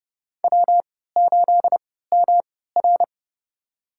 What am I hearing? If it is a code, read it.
Morse code, heard as a keyed tone, about 32 wpm, spaced as for 17 wpm, 700 Hz. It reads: W8MR
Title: USA Call Signs - 3x Character Spacing - 30wpm